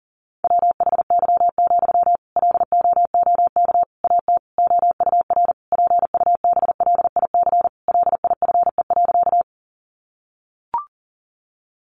Morse code, heard as a keyed tone, about 40 wpm, 700 Hz. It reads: WHY, LOOK AT OUR PUBLIC LIFE.
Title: Sentences from Top 600 Words 40wpm